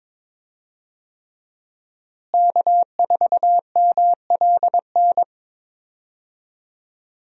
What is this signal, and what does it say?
Morse code, heard as a keyed tone, about 22 wpm, 700 Hz. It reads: K4MLN
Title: USA Call Signs - 5 Characters - Repeated 3x - 22wpm